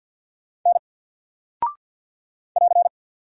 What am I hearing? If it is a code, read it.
Morse code, heard as a keyed tone, about 50 wpm, 700 Hz. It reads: N <AR>
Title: Single Character (Letter:Number:Punctuation:Prosign) 50wpm